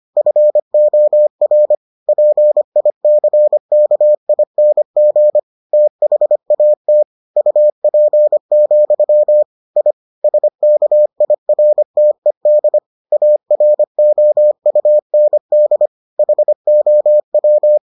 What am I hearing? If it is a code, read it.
Morse code, heard as a keyed tone, about 25 wpm, 600 Hz. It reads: FOR PICKING THAT UP, I SKIRTED AROUND HOW